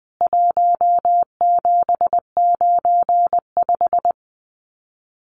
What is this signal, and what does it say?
Morse code, heard as a keyed tone, about 20 wpm, 700 Hz. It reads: 1795
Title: CWT Exchange 20wpm